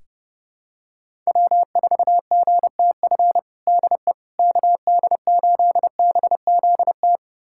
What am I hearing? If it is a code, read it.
Morse code, heard as a keyed tone, about 32 wpm, 700 Hz. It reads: W4GTF DE KD8BZT
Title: USA Callsign DE Callsign 30wpm